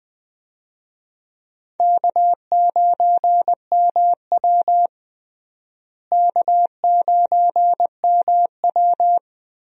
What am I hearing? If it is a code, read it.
Morse code, heard as a keyed tone, about 20 wpm, 700 Hz. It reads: K9MW K9MW
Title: USA Call Signs - 4 Characters - Repeated 2x - 20wpm